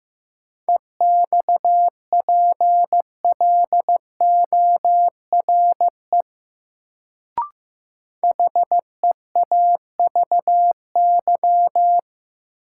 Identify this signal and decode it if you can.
Morse code, heard as a keyed tone, about 15 wpm, 700 Hz. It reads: EXPLORE HEAVY
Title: Sentences from Top 1100 Words - New Word Drill Alt - 15wpm